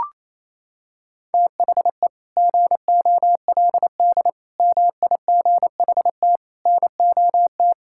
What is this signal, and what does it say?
Morse code, heard as a keyed tone, about 28 wpm, 700 Hz. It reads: THE GOLD MIGHT NOT
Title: Sentences from Top 1000 Words - Review Part 5 - 28wpm